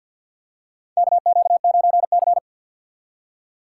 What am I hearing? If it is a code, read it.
Morse code, heard as a keyed tone, about 52 wpm, 700 Hz. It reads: KQ9C
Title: USA Call Signs - 4 Characters - 6x Character Spacing - 50wpm